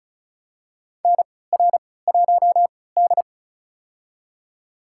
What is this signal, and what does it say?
Morse code, heard as a keyed tone, about 36 wpm, spaced as for 20 wpm, 700 Hz. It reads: NR1D
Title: USA Call Signs - 4 Characters - 3x Character Spacing - 35wpm